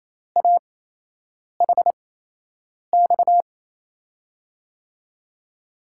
Morse code, heard as a keyed tone, about 28 wpm, spaced as for 8 wpm, 700 Hz. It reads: AHX
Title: USA Call Signs - 6 Characters - 8x Character Spacing - 28wpm